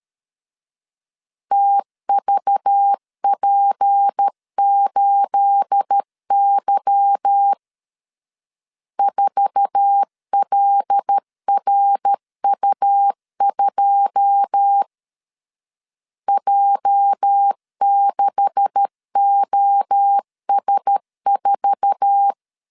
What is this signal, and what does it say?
Morse code, heard as a keyed tone, about 13 wpm, 800 Hz. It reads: TVP8Y 4LRU2 J6OS4
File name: CW / DET / 1 9 1500C S A 12wpm 3800